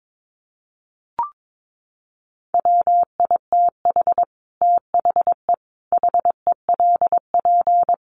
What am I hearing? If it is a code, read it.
Morse code, heard as a keyed tone, about 22 wpm, 700 Hz. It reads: WITH THE HELP
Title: Sets of 4 Words 22wpm